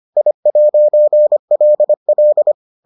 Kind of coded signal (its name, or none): Morse code